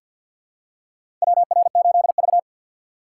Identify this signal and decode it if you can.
Morse code, heard as a keyed tone, about 47 wpm, 700 Hz. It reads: WR8V
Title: USA Call Signs - 4 Characters - 3x Character Spacing - 50wpm